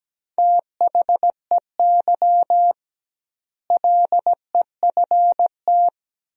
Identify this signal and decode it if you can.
Morse code, heard as a keyed tone, about 17 wpm, 700 Hz. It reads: THEY LEFT
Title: 2x Word-Spacing - Sentences from Top 400 Words - 17wpm